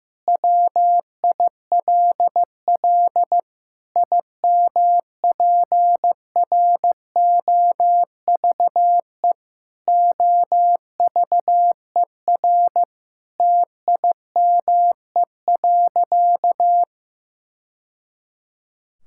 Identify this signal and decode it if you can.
Morse code, heard as a keyed tone, about 14 wpm, 700 Hz. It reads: WILL IMPROVE OVER TIME.
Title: Sentences from Top 1000 Words - Review Part 5 - 15wpm